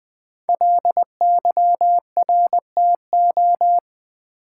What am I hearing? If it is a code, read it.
Morse code, heard as a keyed tone, about 20 wpm, 700 Hz. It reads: LYRTO
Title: Code Groups - 5 Letters - 20wpm